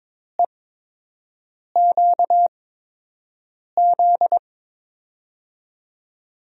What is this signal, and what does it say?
Morse code, heard as a keyed tone, about 22 wpm, 700 Hz. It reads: E Q Z
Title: USA Call Signs - 6 Characters - 8x Character Spacing - 22wpm